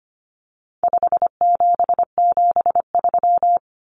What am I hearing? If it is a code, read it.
Morse code, heard as a keyed tone, about 25 wpm, 700 Hz. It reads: 5773
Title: Common Cut Numbers - 4-Digit 25wpm